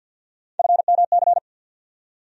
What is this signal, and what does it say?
Morse code, heard as a keyed tone, about 50 wpm, 700 Hz. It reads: FMC
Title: USA Call Sign Postfixes - 3 Letters 50wpm